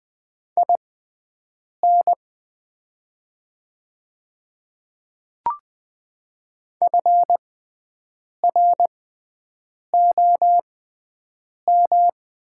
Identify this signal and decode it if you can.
Morse code, heard as a keyed tone, about 20 wpm, spaced as for 7 wpm, 700 Hz. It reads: IN FROM
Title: 6x Character Spacing - Top 100 Words - 20wpm